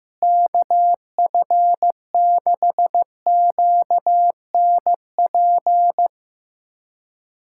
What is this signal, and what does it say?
Morse code, heard as a keyed tone, about 15 wpm, 700 Hz. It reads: KF6QNP
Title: USA Callsign DE Callsign - 6 Characters - 15wpm